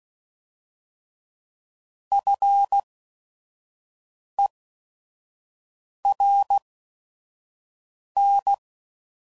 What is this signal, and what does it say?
Morse code, heard as a keyed tone, about 16 wpm, spaced as for 5 wpm, 800 Hz. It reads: FERN